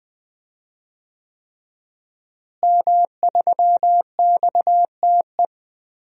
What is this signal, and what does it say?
Morse code, heard as a keyed tone, about 20 wpm, 700 Hz. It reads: M3XTE
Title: DXCC-100 Call Signs - 5 Characters - Repeated 3x - 20wpm